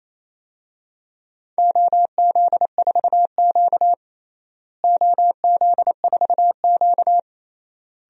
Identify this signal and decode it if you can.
Morse code, heard as a keyed tone, about 28 wpm, 700 Hz. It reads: OZ4Q OZ4Q
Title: DXCC-100 Call Signs - 4 Characters - Repeated 3x - 28wpm